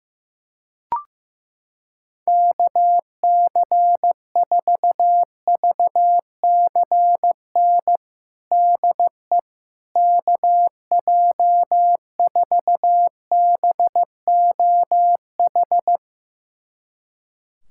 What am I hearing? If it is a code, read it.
Morse code, heard as a keyed tone, about 15 wpm, 700 Hz. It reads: KC4VCN DE KJ4BOH
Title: USA Callsign DE Callsign - 6 Characters - 15wpm